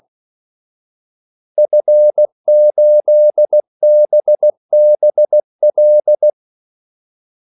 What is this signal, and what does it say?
Morse code, heard as a keyed tone, about 16 wpm, 600 Hz. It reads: F8BBL
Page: Kaldesignaler 16 wpm